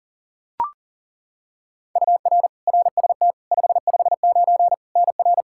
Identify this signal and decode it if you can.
Morse code, heard as a keyed tone, about 40 wpm, 700 Hz. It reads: UR RST 559 NR
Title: Practice QSO - Exchange 1 - Beginning - 40wpm